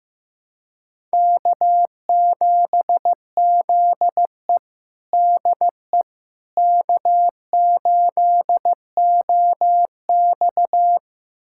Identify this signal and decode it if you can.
Morse code, heard as a keyed tone, about 15 wpm, 700 Hz. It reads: K7ZE DE K8OX
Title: USA Callsign DE Callsign - 4 Characters - 15wpm